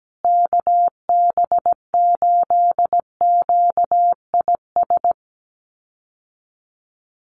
Morse code, heard as a keyed tone, about 17 wpm, 700 Hz. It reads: KB8QIS
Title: USA Call Signs Repeated 3x 17wpm